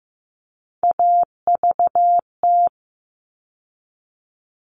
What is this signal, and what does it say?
Morse code, heard as a keyed tone, about 15 wpm, 700 Hz. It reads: AVT
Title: All Cut Numbers - 3-Digit 15wpm